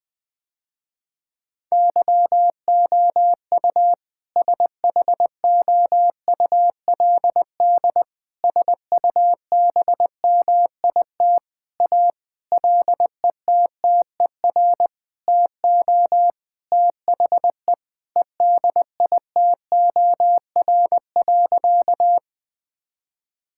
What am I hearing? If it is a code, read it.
Morse code, heard as a keyed tone, about 21 wpm, 700 Hz. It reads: YOU SHOULD SUBMIT A LETTER TO THE EDITOR.
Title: Sentences with 6-Word Phrases - Spoken Beginning - 20wpm